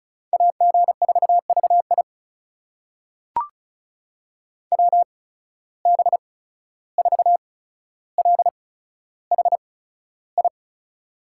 Morse code, heard as a keyed tone, about 36 wpm, spaced as for 14 wpm, 700 Hz. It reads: AG4VI WB4LHI
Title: USA Call Signs - 8x Character Spacing - 35wpm